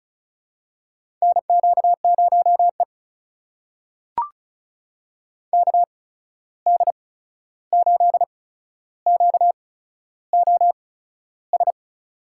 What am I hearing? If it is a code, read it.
Morse code, heard as a keyed tone, about 36 wpm, spaced as for 13 wpm, 700 Hz. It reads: NQ0E KD8QOS
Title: USA Call Signs - 8x Character Spacing - 35wpm